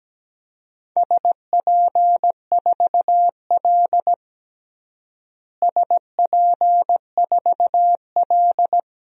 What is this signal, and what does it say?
Morse code, heard as a keyed tone, about 17 wpm, 700 Hz. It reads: SP4L SP4L
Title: DXCC-100 Call Signs - 4 Characters - Repeated 3x - 17wpm